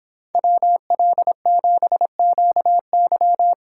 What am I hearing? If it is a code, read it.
Morse code, heard as a keyed tone, about 28 wpm, 700 Hz. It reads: WL7QY